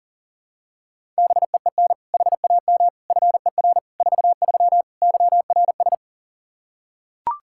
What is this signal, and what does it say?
Morse code, heard as a keyed tone, about 40 wpm, 700 Hz. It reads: BEEN HAM FER 43 YRS
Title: Practice QSO - Exchange 3 - More Info - 40wpm